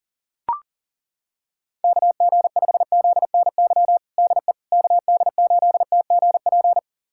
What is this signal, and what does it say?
Morse code, heard as a keyed tone, about 37 wpm, 700 Hz. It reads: KG5ZNY DE KD8TGP
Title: USA Callsign DE Callsign - 6 Characters - 40wpm